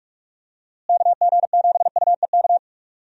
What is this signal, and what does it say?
Morse code, heard as a keyed tone, about 42 wpm, 700 Hz. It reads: KG7UEK
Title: USA Call Signs - 6 Characters - Repeated 3x - 45wpm